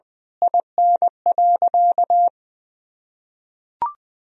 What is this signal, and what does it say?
Morse code, heard as a keyed tone, about 21 wpm, 700 Hz. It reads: IN.
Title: Sentences from Top 1000 Words - Review Part 4 - 20wpm